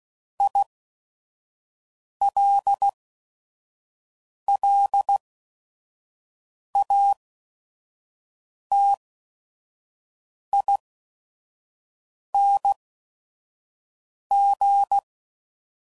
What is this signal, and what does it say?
Morse code, heard as a keyed tone, about 16 wpm, spaced as for 5 wpm, 800 Hz. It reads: ILLATING
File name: Suffixes / Special ING's / VerySlowToFaster ING's